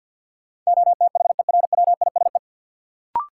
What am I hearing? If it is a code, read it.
Morse code, heard as a keyed tone, about 53 wpm, 700 Hz. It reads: OTHERWISE
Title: Sentences from Top 1200 Words - New Word Drill Alt - 50wpm